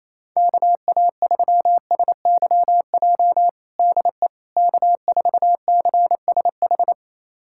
Morse code, heard as a keyed tone, about 28 wpm, 700 Hz. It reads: KA3SYJ DE K4CSH
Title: USA Callsign DE Callsign 28wpm